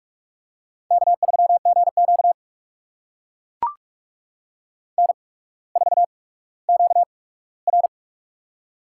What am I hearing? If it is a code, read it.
Morse code, heard as a keyed tone, about 44 wpm, spaced as for 19 wpm, 700 Hz. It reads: K3GQ N4QR
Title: USA Call Signs - 4 Characters - 8x Character Spacing - 45wpm